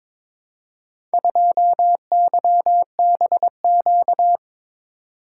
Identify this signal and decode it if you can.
Morse code, heard as a keyed tone, about 22 wpm, 700 Hz. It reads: 2YBQ
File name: practice / 22wpm / Four Letter-Number Combinations - 22wpm